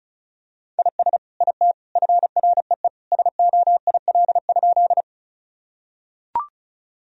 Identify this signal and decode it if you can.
Morse code, heard as a keyed tone, about 36 wpm, 700 Hz. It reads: IS IT FREE SOIL?
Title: Sentences from Top 1000 Words - Review Part 4 - 35wpm